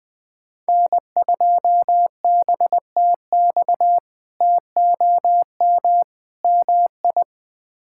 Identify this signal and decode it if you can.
Morse code, heard as a keyed tone, about 20 wpm, 700 Hz. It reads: N2BTX TOM MI
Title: K1USN SST QSO Practice - 20wpm